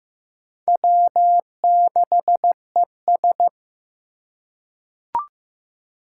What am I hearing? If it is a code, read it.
Morse code, heard as a keyed tone, about 15 wpm, 700 Hz. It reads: W6ES
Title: Four Challenging Letter-Number Combinations 15wpm